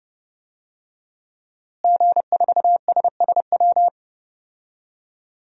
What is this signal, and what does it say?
Morse code, heard as a keyed tone, about 29 wpm, 700 Hz. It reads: G4SSW
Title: DXCC-100 Call Signs - 5 Characters - Repeated 3x - 30wpm